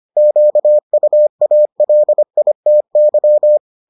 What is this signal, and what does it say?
Morse code, heard as a keyed tone, about 25 wpm, 600 Hz. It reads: QUALITY